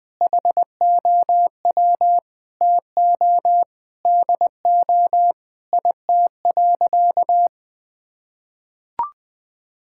Morse code, heard as a keyed tone, about 20 wpm, 700 Hz. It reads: HOW TO DO IT.
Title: Sentences with 8-Word Phrases - 20wpm